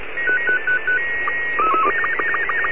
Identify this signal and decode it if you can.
SSTV (slow-scan television), Robot 36.
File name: seeds 03may08 1015utc pa3guo fm 437.485